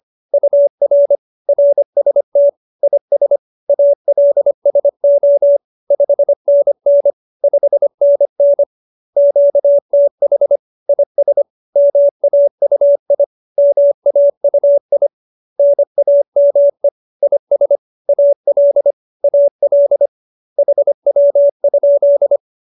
Morse code, heard as a keyed tone, about 25 wpm, 570 Hz. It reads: UR RST IS ALSO 5NN 5NN QTH IS MAUI MAUI NAME IS AL AL HW?